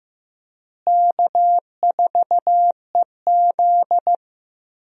Morse code, heard as a keyed tone, about 15 wpm, 700 Hz. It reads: K4EZ
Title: USA Call Signs - 4 Characters - Repeated 3x - 15wpm